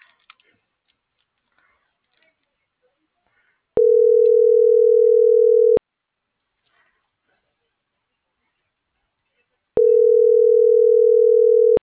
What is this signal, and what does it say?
Ringback tone (North America), 440 + 480 Hz.